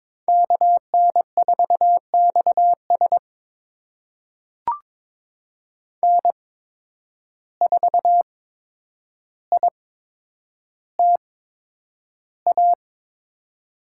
Morse code, heard as a keyed tone, about 23 wpm, spaced as for 10 wpm, 700 Hz. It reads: KN4XS N4ITA
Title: USA Call Signs - 5 Characters - 8x Character Spacing - 22wpm